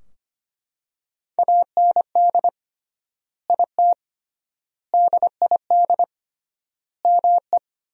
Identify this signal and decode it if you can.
Morse code, heard as a keyed tone, about 25 wpm, 700 Hz. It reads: AND IT DID ME
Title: 3x Word-Spacing - Sentences from Top 200 Words - 25wpm